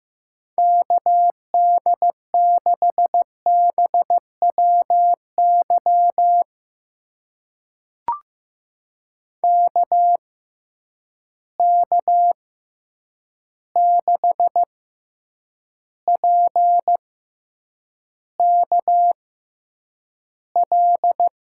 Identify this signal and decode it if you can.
Morse code, heard as a keyed tone, about 15 wpm, spaced as for 8 wpm, 700 Hz. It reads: KD6BWY KK6PKL
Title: USA Call Signs - 6 Characters - 6x Character Spacing - 15wpm